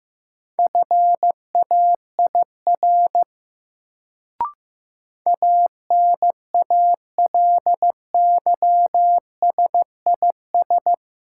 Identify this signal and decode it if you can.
Morse code, heard as a keyed tone, about 15 wpm, 700 Hz. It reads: FAIR ANALYSIS
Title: Sentences from Top 1000 Words - Review Part 5 - New Word Drill - 15wpm